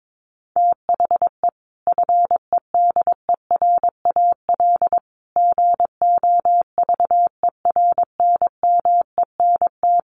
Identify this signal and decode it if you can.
Morse code, heard as a keyed tone, about 22 wpm, 700 Hz. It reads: THE FEDERAL GOVERNMENT
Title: Sentences from Top 1100 Words 22wpm